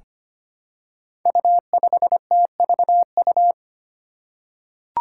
Morse code, heard as a keyed tone, about 25 wpm, 700 Hz. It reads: U5TVU
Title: Five Challenging Letter-Number Combinations 25wpm